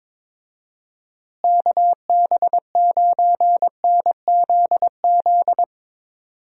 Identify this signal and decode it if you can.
Morse code, heard as a keyed tone, about 22 wpm, 700 Hz. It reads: KB9NZZ
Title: USA Call Signs - 6 Characters - Repeated 2x - 22wpm